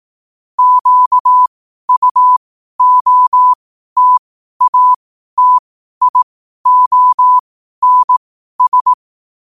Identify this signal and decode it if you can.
Morse code, heard as a keyed tone, about 18 wpm, spaced as for 12 wpm, 1000 Hz. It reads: QUOTATIONS
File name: qotdfiles / 12wpm